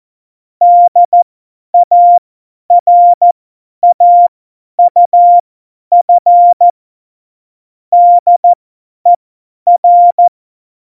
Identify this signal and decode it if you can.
Morse code, heard as a keyed tone, about 14 wpm, spaced as for 10 wpm, 700 Hz. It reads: DARAUF DER
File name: Morsekurs / Witze / Witz 15. Januar 2019, 50 F70